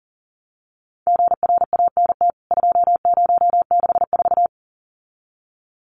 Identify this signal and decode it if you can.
Morse code, heard as a keyed tone, about 40 wpm, 700 Hz. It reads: GRANT 2064
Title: CWT Exchange 40wpm